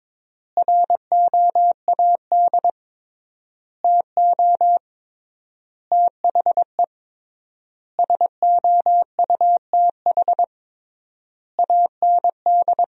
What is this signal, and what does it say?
Morse code, heard as a keyed tone, about 22 wpm, 700 Hz. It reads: ROAD TO THE SOUTH AND
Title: 3x Word-Spacing - Sentences from Top 400 Words - 22wpm